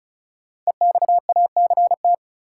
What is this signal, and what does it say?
Morse code, heard as a keyed tone, about 36 wpm, 700 Hz. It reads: EXACT
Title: Sentences from Top 1300 Words - New Word Drill Alt - 35wpm